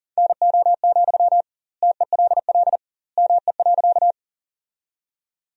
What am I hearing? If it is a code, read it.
Morse code, heard as a keyed tone, about 40 wpm, 700 Hz. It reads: NO, TELL ME.
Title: 2x Word-Spacing - Sentences from Top 200 Words - 40wpm